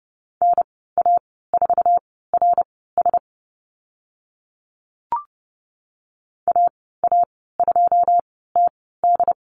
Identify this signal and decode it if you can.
Morse code, heard as a keyed tone, about 30 wpm, spaced as for 17 wpm, 700 Hz. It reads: NA4RS AA2TD
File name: Lesson 15g - Number 5 - Call Signs 30wpm